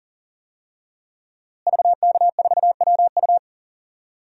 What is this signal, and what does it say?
Morse code, heard as a keyed tone, about 39 wpm, 700 Hz. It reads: VK4WU
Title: DXCC-100 Call Signs - 5 Characters - Repeated 3x - 40wpm